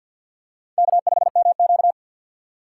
Morse code, heard as a keyed tone, about 51 wpm, 700 Hz. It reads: K5MQ
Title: USA Call Signs - 4 Characters - 6x Character Spacing - 50wpm